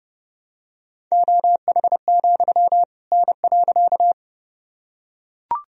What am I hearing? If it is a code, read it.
Morse code, heard as a keyed tone, about 30 wpm, 700 Hz. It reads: OH, N.
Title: Sentences from Top 500 Words and Abbreviations 30wpm